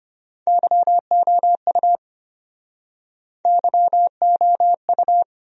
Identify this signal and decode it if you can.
Morse code, heard as a keyed tone, about 34 wpm, 700 Hz. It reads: YOU YOU